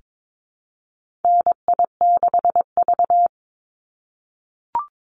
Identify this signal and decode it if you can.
Morse code, heard as a keyed tone, about 22 wpm, 700 Hz. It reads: NI6V
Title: Call Signs - 22wpm